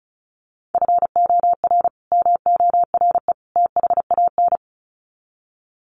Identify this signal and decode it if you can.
Morse code, heard as a keyed tone, about 34 wpm, 700 Hz. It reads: FOR MORE THAN
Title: Sets of 3 Words 35wpm